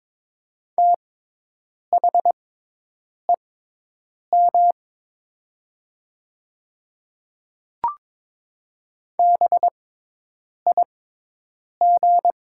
Morse code, heard as a keyed tone, about 21 wpm, spaced as for 8 wpm, 700 Hz. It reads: THEM BIG
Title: 6x Character Spacing - Top 300 Words - 22wpm